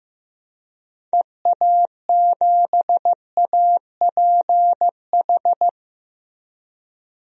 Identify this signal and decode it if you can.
Morse code, heard as a keyed tone, about 16 wpm, 700 Hz. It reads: EA7APH
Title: DXCC-100 Call Signs - 6 Characters - Repeated 3x - 15wpm